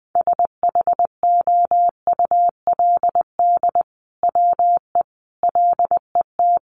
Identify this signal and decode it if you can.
Morse code, heard as a keyed tone, about 20 wpm, 700 Hz. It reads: SHOULD WE LET
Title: Sentences from Top 600 Words 20wpm